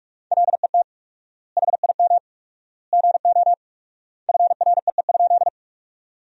Morse code, heard as a keyed tone, about 42 wpm, 690 Hz. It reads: LET HIM GO FREE?
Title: 4x Word-Spacing - Sentences from Top 500 Words - 45wpm